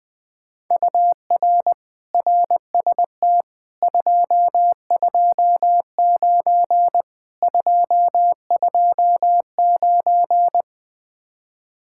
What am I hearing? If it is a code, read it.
Morse code, heard as a keyed tone, about 20 wpm, 700 Hz. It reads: UR RST 229 229
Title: Speed-Racing Common QSO Phrases Encore - 17wpm